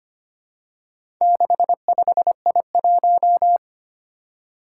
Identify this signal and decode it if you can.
Morse code, heard as a keyed tone, about 25 wpm, 700 Hz. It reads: 65I1
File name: practice / 25wpm / Four Letter-Number Combinations - 25wpm